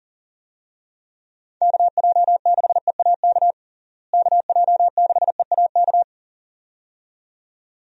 Morse code, heard as a keyed tone, about 37 wpm, 700 Hz. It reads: KJ6EAK KJ6EAK
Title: USA Call Signs - 6 Characters - Repeated 2x - 40wpm